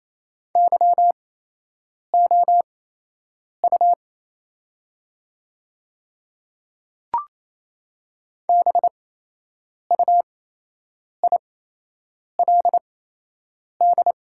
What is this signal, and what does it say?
Morse code, heard as a keyed tone, about 27 wpm, spaced as for 8 wpm, 700 Hz. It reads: YOU BUILD
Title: 8x Character Spacing - Top 200 Words - 28wpm